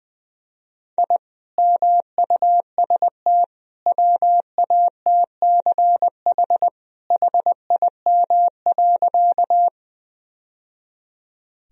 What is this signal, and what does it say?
Morse code, heard as a keyed tone, about 21 wpm, 700 Hz. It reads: I MUST WATCH HIM.
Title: Sentences from Top 1000 Words - Review Part 1 - 20wpm